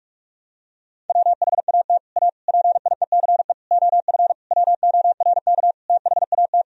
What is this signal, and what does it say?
Morse code, heard as a keyed tone, about 46 wpm, 700 Hz. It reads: WHAT A PIECE OF WORK THAT
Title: Sentences from Top 1000 Words - Review Part 2 - 45wpm